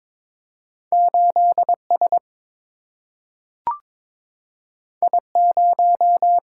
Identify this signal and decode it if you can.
Morse code, heard as a keyed tone, about 22 wpm, 700 Hz. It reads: 8S I0
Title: Single-Letter Single-Number 22wpm